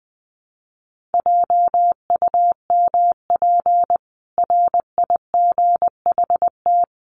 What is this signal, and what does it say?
Morse code, heard as a keyed tone, about 19 wpm, 700 Hz. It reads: JUMP RIGHT
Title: Sentences from Top 1300 Words 20wpm